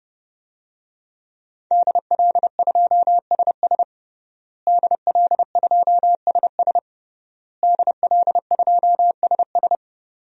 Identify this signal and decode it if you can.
Morse code, heard as a keyed tone, about 32 wpm, 700 Hz. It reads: DL2SS DL2SS DL2SS
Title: DXCC-100 Call Signs - 5 Characters - Repeated 3x - 30wpm